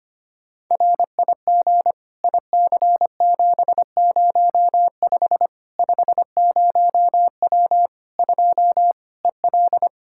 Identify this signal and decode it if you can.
Morse code, heard as a keyed tone, about 25 wpm, 700 Hz. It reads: RIG IC705 50W 2 EL
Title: Practice QSO - Exchange 2 - Info - 25wpm